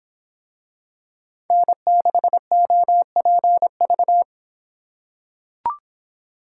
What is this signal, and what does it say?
Morse code, heard as a keyed tone, about 23 wpm, 700 Hz. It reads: N6OPV